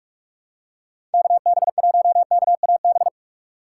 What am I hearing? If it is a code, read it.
Morse code, heard as a keyed tone, about 46 wpm, 700 Hz. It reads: KB1KAB
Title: USA Call Signs - 6 Characters - 2x Character Spacing - 45wpm